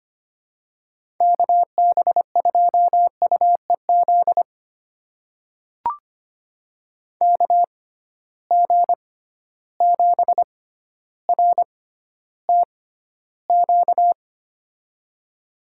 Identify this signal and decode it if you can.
Morse code, heard as a keyed tone, about 25 wpm, spaced as for 13 wpm, 700 Hz. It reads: KB2UEZ KG7RTQ
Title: USA Call Signs - 6 Characters - 6x Character Spacing - 25wpm